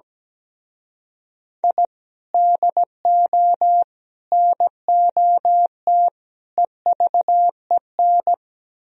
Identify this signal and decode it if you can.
Morse code, heard as a keyed tone, about 17 wpm, 700 Hz. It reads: I DO NOT EVEN
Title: Speed-Racing - Sets of 4 Words 17wpm